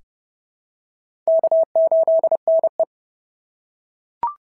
Morse code, heard as a keyed tone, about 30 wpm, 650 Hz. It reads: K8NE
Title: SOTA QSO Practice 30wpm